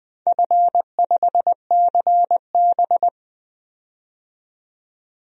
Morse code, heard as a keyed tone, about 20 wpm, 700 Hz. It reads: F5CB
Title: DXCC-100 Call Signs - 4 Characters - 20wpm